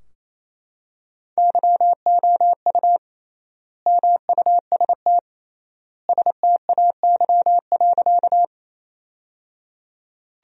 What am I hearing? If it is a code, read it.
Morse code, heard as a keyed tone, about 28 wpm, 700 Hz. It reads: YOU MUST STAY.
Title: 3x Word-Spacing - Sentences from Top 500 Words - 28wpm